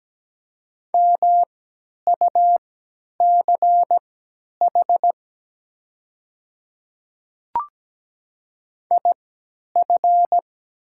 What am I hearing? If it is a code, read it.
Morse code, heard as a keyed tone, about 17 wpm, spaced as for 10 wpm, 700 Hz. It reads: MUCH IF
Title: 3x Character Spacing - Top 200 Words - 17wpm